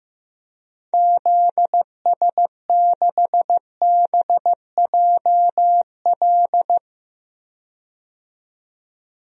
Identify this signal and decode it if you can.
Morse code, heard as a keyed tone, about 15 wpm, 700 Hz. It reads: ZS6BJL
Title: DXCC-100 Call Signs - 6 Characters - 15wpm